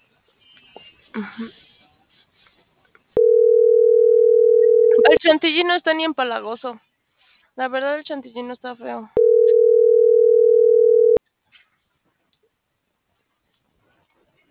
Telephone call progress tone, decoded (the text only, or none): ringback tone